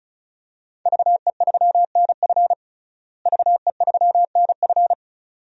Morse code, heard as a keyed tone, about 36 wpm, 700 Hz. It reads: VE3NF VE3NF
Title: DXCC-100 Call Signs - 5 Characters - Repeated 2x - 35wpm